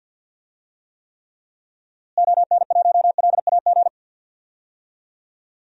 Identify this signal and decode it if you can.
Morse code, heard as a keyed tone, about 52 wpm, 700 Hz. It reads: ON1LAG
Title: DXCC-100 Call Signs - 6 Characters - Repeated 3x - 50wpm